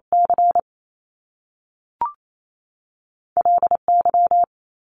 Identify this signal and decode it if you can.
Morse code, heard as a keyed tone, about 28 wpm, 700 Hz. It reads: C LY
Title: Top Two Letter Combinations in Words 28wpm